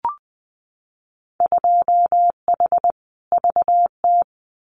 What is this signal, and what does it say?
Morse code, heard as a keyed tone, about 20 wpm, 700 Hz. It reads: 2H VT
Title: Winter Field Day Exchanges 20wpm